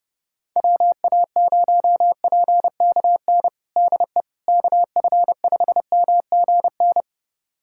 Morse code, heard as a keyed tone, about 30 wpm, 700 Hz. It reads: WA0PKN DE KF5MGN
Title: USA Callsign DE Callsign - 6 Characters - 30wpm